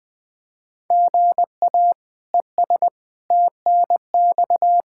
Morse code, heard as a keyed tone, about 21 wpm, 700 Hz. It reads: GA ES TNX
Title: Practice QSO - Exchange 1 - Beginning - 20wpm